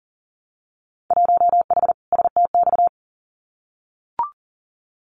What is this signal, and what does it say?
Morse code, heard as a keyed tone, about 41 wpm, 700 Hz. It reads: 1H STX
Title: Winter Field Day Exchanges Encore 40wpm